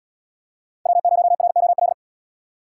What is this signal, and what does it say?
Morse code, heard as a keyed tone, about 71 wpm, 700 Hz. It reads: W0ROL